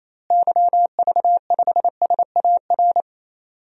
Morse code, heard as a keyed tone, about 28 wpm, 700 Hz. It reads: YV5SAR
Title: DXCC-100 Call Signs - 6 Characters - Repeated 3x - 28wpm